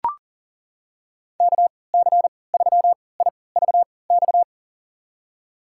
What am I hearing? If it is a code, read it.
Morse code, heard as a keyed tone, about 38 wpm, spaced as for 22 wpm, 700 Hz. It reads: KC3IVX
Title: USA Call Signs - 6 Characters - 3x Character Spacing - 40wpm